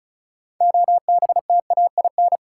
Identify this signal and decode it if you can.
Morse code, heard as a keyed tone, about 34 wpm, 700 Hz. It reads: OBTAIN
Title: Sentences from Top 1300 Words - New Word Drill Alt - 35wpm